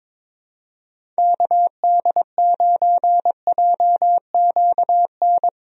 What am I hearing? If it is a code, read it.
Morse code, heard as a keyed tone, about 22 wpm, 700 Hz. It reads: KD9JQN
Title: USA Call Signs - 6 Characters - 22wpm